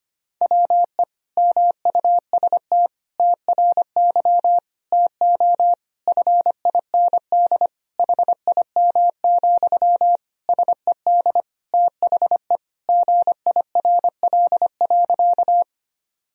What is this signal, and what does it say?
Morse code, heard as a keyed tone, about 25 wpm, 700 Hz. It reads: WE MUST TRY TO FIND HIM, SED THE GIRL.
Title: Sentences from Top 400 Words and Abbreviations 25wpm